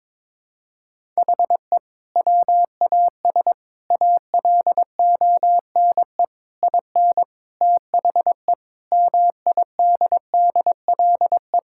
Morse code, heard as a keyed tone, about 22 wpm, 700 Hz. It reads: HE WAS ALONE IN THE MIDDLE
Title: Sentences from Top 1000 Words - Review Part 6 - 22wpm